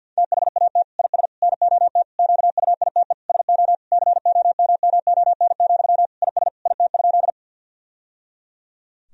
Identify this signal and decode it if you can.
Morse code, heard as a keyed tone, about 52 wpm, 700 Hz. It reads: THAT IS NOT QUITE SO COMMON, IS IT?
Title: Sentences from Top 1000 Words - Review Part 3 - 50wpm